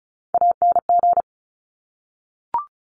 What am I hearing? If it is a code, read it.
Morse code, heard as a keyed tone, about 36 wpm, 700 Hz. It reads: ANG
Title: Top Three Letter Combinations in Words 35wpm